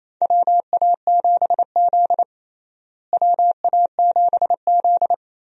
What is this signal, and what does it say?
Morse code, heard as a keyed tone, about 28 wpm, 700 Hz. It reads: WA7Z WA7Z
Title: USA Call Signs - 4 Characters - Repeated 2x - 28wpm